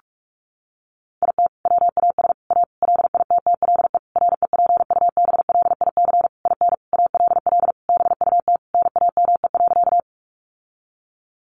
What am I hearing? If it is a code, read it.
Morse code, heard as a keyed tone, about 44 wpm, 700 Hz. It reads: IT WAS A LITTLE REPUBLIC IN ALL BUT NAME.
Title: Sentences from Top 1500 Words 45wpm